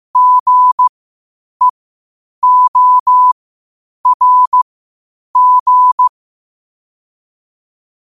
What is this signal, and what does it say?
Morse code, heard as a keyed tone, about 15 wpm, spaced as for 8 wpm, 1000 Hz. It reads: GEORG
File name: qotdfiles / 8wpm